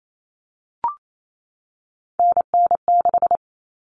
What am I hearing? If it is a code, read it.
Morse code, heard as a keyed tone, about 28 wpm, 700 Hz. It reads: NN6
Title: Common Cut Numbers - 3-Digit 28wpm